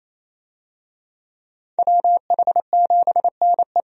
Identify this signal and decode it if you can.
Morse code, heard as a keyed tone, about 28 wpm, 700 Hz. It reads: WH7NE
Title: USA Call Signs - 5 Characters - 28wpm